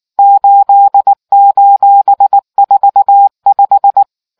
Morse code, heard as a keyed tone, about 19 wpm, 800 Hz. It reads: 8:45